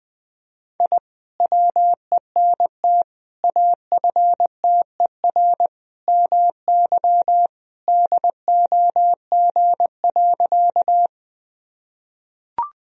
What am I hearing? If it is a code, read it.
Morse code, heard as a keyed tone, about 20 wpm, 700 Hz. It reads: I WENT AFTER MY DOG.
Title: Sentences from Top 1000 Words - Review Part 2 - 20wpm